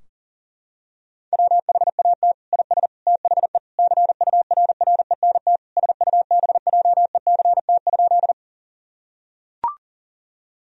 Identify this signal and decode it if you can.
Morse code, heard as a keyed tone, about 40 wpm, 700 Hz. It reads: WHAT IS THE CURRENT SUBJECT?
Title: Sentences from Top 1000 Words - Review Part 4 - 40wpm